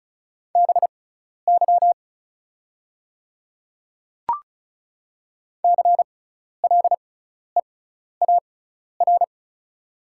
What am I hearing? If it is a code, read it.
Morse code, heard as a keyed tone, about 36 wpm, spaced as for 12 wpm, 690 Hz. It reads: BY CLEAR
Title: 6x Character Spacing - Top 500 Words - 35wpm